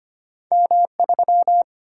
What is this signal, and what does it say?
Morse code, heard as a keyed tone, about 25 wpm, 700 Hz. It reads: M3
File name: Two Letter-Number Combinations - 25wpm